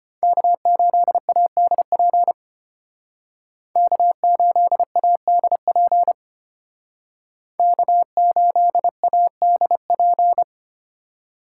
Morse code, heard as a keyed tone, about 31 wpm, 700 Hz. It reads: K8ADP K8ADP K8ADP
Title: Speed Racing - USA 1x3 Call Signs 25wpm